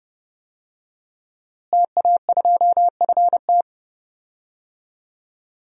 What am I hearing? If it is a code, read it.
Morse code, heard as a keyed tone, about 32 wpm, 700 Hz. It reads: TA2FT
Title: DXCC-100 Call Signs - 5 Characters - Repeated 2x - 30wpm